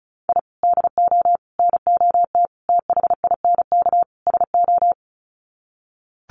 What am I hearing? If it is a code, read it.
Morse code, heard as a keyed tone, about 34 wpm, 700 Hz. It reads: I DO NOT THINK SO
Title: Sets of 5 Words 35wpm